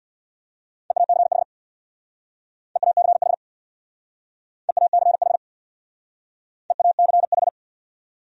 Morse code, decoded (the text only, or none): EACH EACH EACH EACH